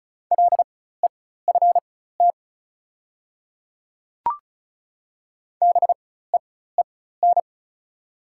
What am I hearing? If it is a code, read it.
Morse code, heard as a keyed tone, about 34 wpm, spaced as for 16 wpm, 700 Hz. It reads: LEFT BEEN
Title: 4x Character Spacing - Top 300 Words - 35wpm